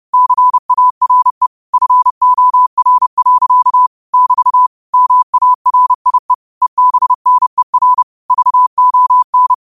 Morse code, heard as a keyed tone, about 30 wpm, 1000 Hz. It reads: CARE FOR. <BT> MARIE EBNER VON
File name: qotdfiles / 30wpm